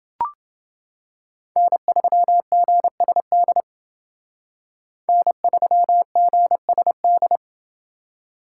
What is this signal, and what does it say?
Morse code, heard as a keyed tone, about 29 wpm, 700 Hz. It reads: N3GSD N3GSD